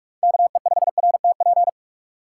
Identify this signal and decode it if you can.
Morse code, heard as a keyed tone, about 44 wpm, 700 Hz. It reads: KE5RTP